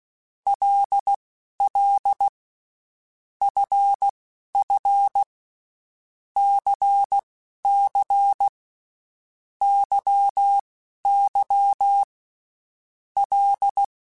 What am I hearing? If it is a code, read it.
Morse code, heard as a keyed tone, about 16 wpm, spaced as for 11 wpm, 800 Hz. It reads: LL FF CC YY L
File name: Groups@10 / Groups@10 mp3s / LFCYQ@10